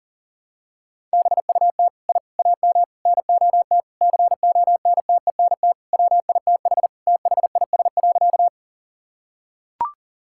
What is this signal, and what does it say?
Morse code, heard as a keyed tone, about 37 wpm, 700 Hz. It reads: BUT I AM NOT CONTENT WITH THIS.
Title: Sentences from Top 1000 Words - Review Part 6 - 40wpm